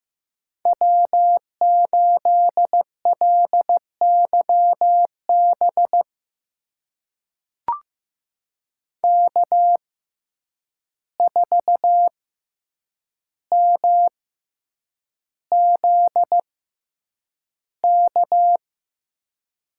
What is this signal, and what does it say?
Morse code, heard as a keyed tone, about 15 wpm, spaced as for 8 wpm, 700 Hz. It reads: W8LYB K4MZK
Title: USA Call Signs - 5 Characters - 6x Character Spacing - 15wpm